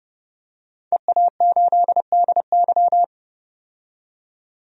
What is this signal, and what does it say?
Morse code, heard as a keyed tone, about 30 wpm, 700 Hz. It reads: EA8DY
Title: DXCC-100 Call Signs - 5 Characters - 30wpm